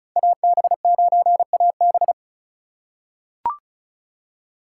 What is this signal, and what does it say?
Morse code, heard as a keyed tone, about 36 wpm, 700 Hz. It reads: AB9AB
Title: USA Call Signs - 5 Characters - 2x Character Spacing - 35wpm